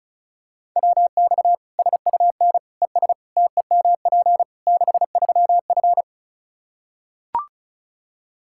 Morse code, heard as a keyed tone, about 36 wpm, 700 Hz. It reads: WX SUN ES TEMP 63F
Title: Practice QSO - Exchange 2 - Info - 35wpm